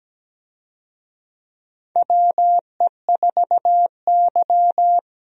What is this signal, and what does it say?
Morse code, heard as a keyed tone, about 17 wpm, 700 Hz. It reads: WE4Y
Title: USA Call Signs - 4 Characters - 17wpm